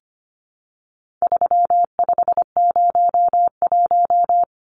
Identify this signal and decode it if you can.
Morse code, heard as a keyed tone, about 25 wpm, 700 Hz. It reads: 3501
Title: Numbers - 4-Digit 25wpm